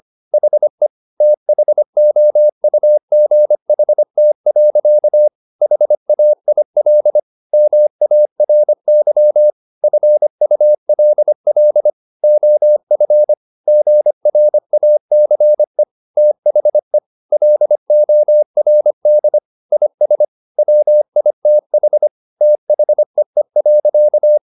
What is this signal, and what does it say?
Morse code, heard as a keyed tone, about 25 wpm, 600 Hz. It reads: HE THOUGHT. HAIL MARY FULL OF GRACE THE LORD IS WITH THEE.
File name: the old man and the sea part 1 25wpm